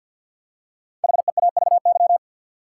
Morse code, heard as a keyed tone, about 50 wpm, 700 Hz. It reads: HEAVY
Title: Five Letter Words - 50wpm